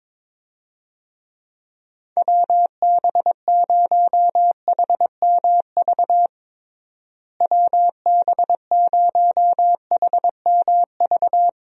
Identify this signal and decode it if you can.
Morse code, heard as a keyed tone, about 22 wpm, 700 Hz. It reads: WB0HMV WB0HMV
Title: USA Call Signs - 6 Characters - Repeated 3x - 22wpm